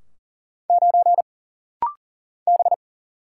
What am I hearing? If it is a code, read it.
Morse code, heard as a keyed tone, about 40 wpm, 700 Hz. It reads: 9 B
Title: Single Letter-Number - Rapid-Fire 40wpm